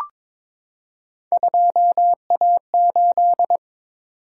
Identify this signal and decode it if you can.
Morse code, heard as a keyed tone, about 22 wpm, 700 Hz. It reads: 2A8
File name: practice / 22wpm / Three Letter-Number Combinations - 22wpm